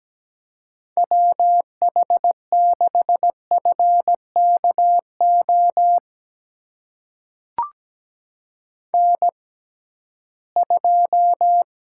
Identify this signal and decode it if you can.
Morse code, heard as a keyed tone, about 17 wpm, spaced as for 13 wpm, 700 Hz. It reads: WH6FKO N2
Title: USA Call Signs - 6x Character Spacing - 17wpm